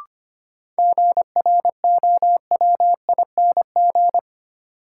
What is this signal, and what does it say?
Morse code, heard as a keyed tone, about 25 wpm, 700 Hz. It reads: GROWING
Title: Sets of 5 Words Encore - New Word Drill - 25wpm